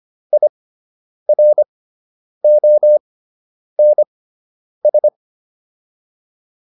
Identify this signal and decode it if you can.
Morse code, heard as a keyed tone, about 25 wpm, spaced as for 9 wpm, 600 Hz. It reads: IRONS